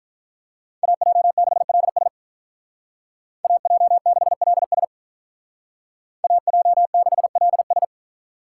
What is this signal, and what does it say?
Morse code, heard as a keyed tone, about 46 wpm, 700 Hz. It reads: AJ6LS AJ6LS AJ6LS